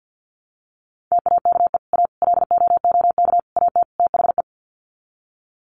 Morse code, heard as a keyed tone, about 50 wpm, 700 Hz. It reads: TAKE A LOOK AT THE
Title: Sets of 5 Words 50wpm